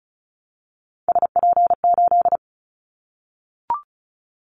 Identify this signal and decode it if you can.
Morse code, heard as a keyed tone, about 34 wpm, 700 Hz. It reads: SP8
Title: Callsign Prefixes for Top 25 DXCC - Repeated 3x - 35wpm